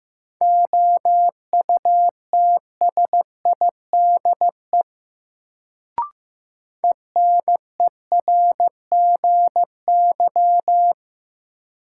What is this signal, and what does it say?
Morse code, heard as a keyed tone, about 15 wpm, 700 Hz. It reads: OUTSIDE ENERGY
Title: Sentences from Top 900 Words - New Word Drill - 15wpm